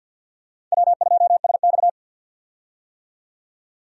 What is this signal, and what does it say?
Morse code, heard as a keyed tone, about 50 wpm, 700 Hz. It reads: W2SX
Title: USA Call Signs - 4 Characters - 50wpm